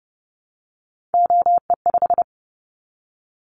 Morse code, heard as a keyed tone, about 31 wpm, 700 Hz. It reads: OE5
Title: Callsign Prefixes for Top 50 DXCC - 30wpm